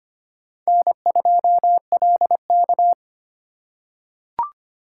Morse code, heard as a keyed tone, about 25 wpm, 700 Hz. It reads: N2LK
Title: K1USN SST QSO Practice - 25wpm